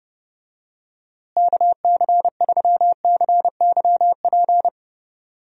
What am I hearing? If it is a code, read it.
Morse code, heard as a keyed tone, about 32 wpm, 700 Hz. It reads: KC3CYP
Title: USA Call Signs - 6 Characters - Repeated 3x - 30wpm